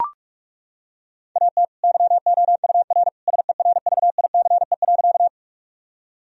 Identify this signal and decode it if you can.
Morse code, heard as a keyed tone, about 41 wpm, 700 Hz. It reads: AT YOUR SERVICE.
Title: Sentences from Top 1000 Words - Review Part 4 - 45wpm